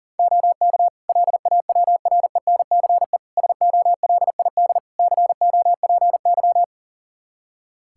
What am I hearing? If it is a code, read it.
Morse code, heard as a keyed tone, about 40 wpm, 700 Hz. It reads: OK LAWRENCE SOLID COPY
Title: Practice QSO - Exchange 3 - More Info - 40wpm